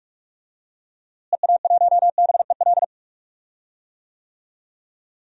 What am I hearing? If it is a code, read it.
Morse code, heard as a keyed tone, about 46 wpm, 700 Hz. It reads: EA1BEL
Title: DXCC-100 Call Signs - 6 Characters - Repeated 2x - 45wpm